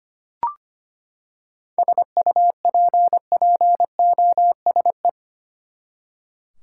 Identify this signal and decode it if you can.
Morse code, heard as a keyed tone, about 25 wpm, 700 Hz. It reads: SUPPOSE
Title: Seven Letter Words - 25wpm